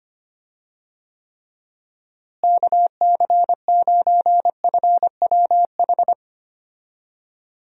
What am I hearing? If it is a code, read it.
Morse code, heard as a keyed tone, about 25 wpm, 700 Hz. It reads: KC9FWH
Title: USA Call Signs - 6 Characters - Repeated 2x - 25wpm